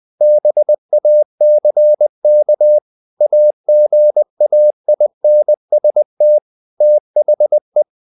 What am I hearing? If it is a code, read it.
Morse code, heard as a keyed tone, about 20 wpm, 600 Hz. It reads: BACK AGAINST THE
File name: the old man and the sea part 2 20wpm